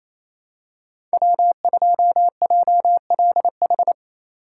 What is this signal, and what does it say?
Morse code, heard as a keyed tone, about 28 wpm, 700 Hz. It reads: W2JLH
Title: USA Call Signs - 5 Characters - 4x Character Spacing - 28wpm